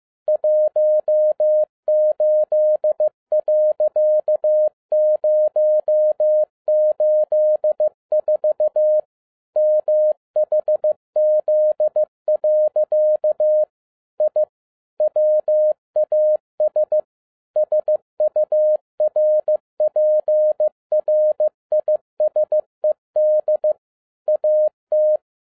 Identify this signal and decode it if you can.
Morse code, heard as a keyed tone, about 15 wpm, 610 Hz. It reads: 18.084 MHZ. I WAS SURPRISED AT
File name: 2011-10-mp3 / n12-JL3YMV 15wpm